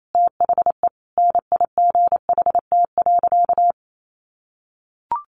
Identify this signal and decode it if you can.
Morse code, heard as a keyed tone, about 28 wpm, 700 Hz. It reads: THE NIGHT.
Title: Sentences from Top 1400 Words 28wpm